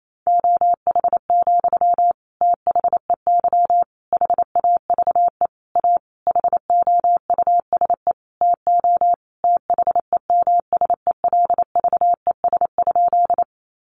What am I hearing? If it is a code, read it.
Morse code, heard as a keyed tone, about 28 wpm, 700 Hz. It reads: OH, THEY HAVE A HOUSE TO THEMSELVES?
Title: Sentences from Top 800 Words 28wpm